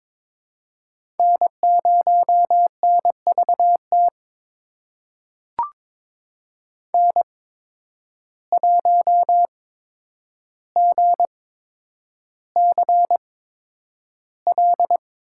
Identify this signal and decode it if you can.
Morse code, heard as a keyed tone, about 23 wpm, spaced as for 10 wpm, 700 Hz. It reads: N0NVT N1GCL
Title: USA Call Signs - 8x Character Spacing - 22wpm